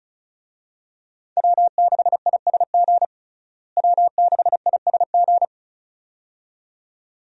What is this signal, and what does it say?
Morse code, heard as a keyed tone, about 34 wpm, 700 Hz. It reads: W6ISG W6ISG
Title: USA Call Signs - 5 Characters - Repeated 2x - 35wpm